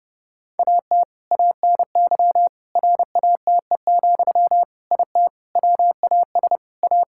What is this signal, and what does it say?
Morse code, heard as a keyed tone, about 30 wpm, 700 Hz. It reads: AT ANY RATE, IT WAS A
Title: Sentences from Top 1600 Words 30wpm